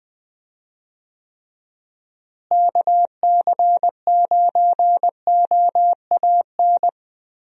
Morse code, heard as a keyed tone, about 20 wpm, 700 Hz. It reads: KC9OAN
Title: USA Call Signs - 6 Characters - Repeated 3x - 20wpm